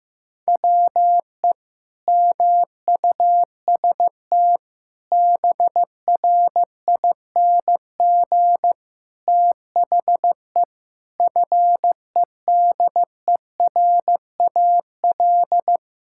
Morse code, heard as a keyed tone, about 15 wpm, 700 Hz. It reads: WE MUST BRING THE FEDERAL
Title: Sentences from Top 1000 Words - Review Part 6 - 15wpm